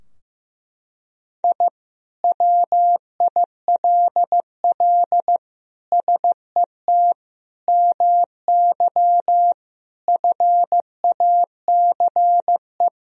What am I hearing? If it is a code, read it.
Morse code, heard as a keyed tone, about 15 wpm, 700 Hz. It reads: I WILL SET MY FACE
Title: Sentences from Top 1000 Words - Review Part 2 - 15wpm